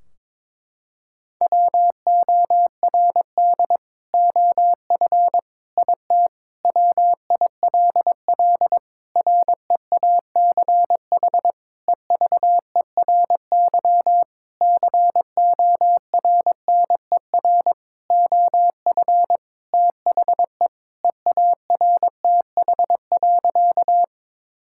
Morse code, heard as a keyed tone, about 21 wpm, 700 Hz. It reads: WORD OF IT WILL REACH EVERY CORNER OF THE EARTH.
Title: Sentences from Top 1600 Words 22wpm